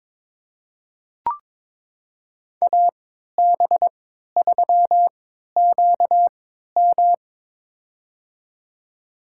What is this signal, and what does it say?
Morse code, heard as a keyed tone, about 22 wpm, spaced as for 12 wpm, 700 Hz. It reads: AB3QM
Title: USA Call Signs - 5 Characters - 3x Character Spacing - 22wpm